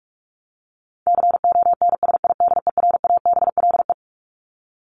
Morse code, heard as a keyed tone, about 44 wpm, 700 Hz. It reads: CONSIDERABLE
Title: LY Suffix - Top 100 Words with Base Words 45wpm